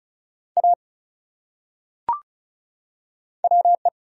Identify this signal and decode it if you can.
Morse code, heard as a keyed tone, about 34 wpm, 700 Hz. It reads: A WE
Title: Top 100 Words 35wpm